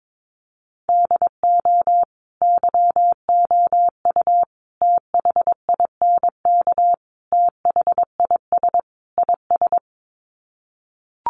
Morse code, heard as a keyed tone, about 22 wpm, 700 Hz. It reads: DO YOU THINK THIS IS
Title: Sets of 5 Words Encore2 22wpm